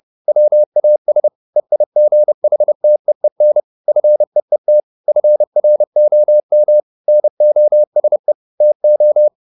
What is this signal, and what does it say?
Morse code, heard as a keyed tone, about 30 wpm, 600 Hz. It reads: WAS EIGHTEEN FEET FROM NOSE TO